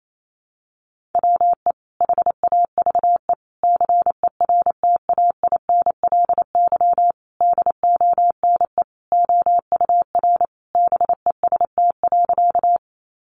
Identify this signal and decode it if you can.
Morse code, heard as a keyed tone, about 28 wpm, 700 Hz. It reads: WE HAVE CERTAINLY DONE OUR BEST.
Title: Sentences from Top 900 Words 28wpm